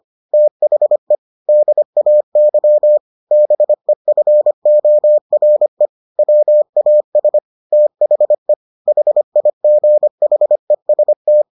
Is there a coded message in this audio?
Yes: Morse code from 0.3 to 11.4 s, heard as a keyed tone, about 25 wpm, 600 Hz. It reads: THE DAY BEFORE WAS THE HIGHEST